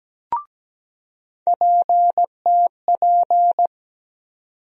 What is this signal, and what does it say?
Morse code, heard as a keyed tone, about 17 wpm, 700 Hz. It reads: PTP
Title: USA Call Sign Postfixes - 3 Letters 17wpm